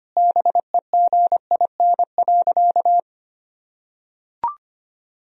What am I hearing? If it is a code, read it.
Morse code, heard as a keyed tone, about 25 wpm, 700 Hz. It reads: BEGIN.
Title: Sentences from Top 300 Words and Abbreviations 25wpm